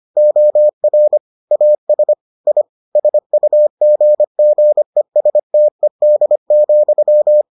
Morse code, heard as a keyed tone, about 25 wpm, 600 Hz. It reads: OR AS I SUGGESTED,